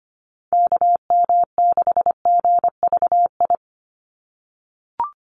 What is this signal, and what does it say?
Morse code, heard as a keyed tone, about 25 wpm, 700 Hz. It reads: KM6GVI
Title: USA Call Signs 25wpm